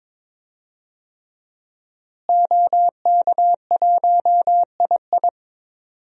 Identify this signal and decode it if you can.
Morse code, heard as a keyed tone, about 22 wpm, 700 Hz. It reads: OK1II
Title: DXCC-100 Call Signs - 5 Characters - Repeated 3x - 22wpm